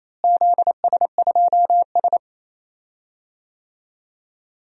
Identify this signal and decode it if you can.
Morse code, heard as a keyed tone, about 27 wpm, 700 Hz. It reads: ZS2S
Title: DXCC-100 Call Signs - 4 Characters - Repeated 2x - 28wpm